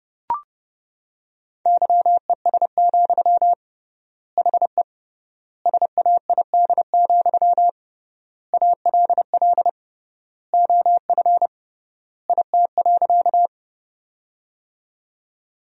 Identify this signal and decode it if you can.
Morse code, heard as a keyed tone, about 30 wpm, 700 Hz. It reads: YES, HE SAID, ALL OF IT.
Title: 3x Word-Spacing - Sentences from Top 500 Words - 30wpm